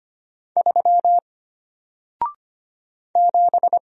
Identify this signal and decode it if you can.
Morse code, heard as a keyed tone, about 25 wpm, 700 Hz. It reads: 3 7
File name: Single-Letter-Number-25wpm